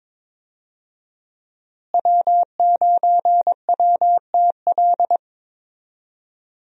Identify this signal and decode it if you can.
Morse code, heard as a keyed tone, about 22 wpm, 700 Hz. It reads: W9WTL
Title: USA Call Signs - 5 Characters - Repeated 2x - 22wpm